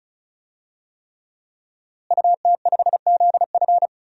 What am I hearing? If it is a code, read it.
Morse code, heard as a keyed tone, about 36 wpm, 700 Hz. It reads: UT5ZF
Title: DXCC-100 Call Signs - 5 Characters - Repeated 3x - 35wpm